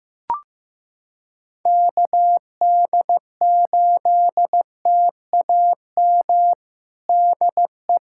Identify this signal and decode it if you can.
Morse code, heard as a keyed tone, about 15 wpm, 700 Hz. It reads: KD8TAM DE
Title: USA Callsign DE Callsign - 6 Characters - 15wpm